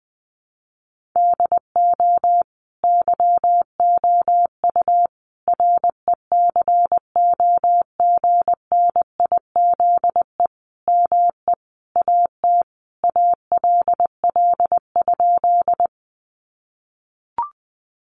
Morse code, heard as a keyed tone, about 20 wpm, 700 Hz. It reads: DO YOU RECOGNIZE ME AT ALL?
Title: Sentences from Top 1400 Words 20wpm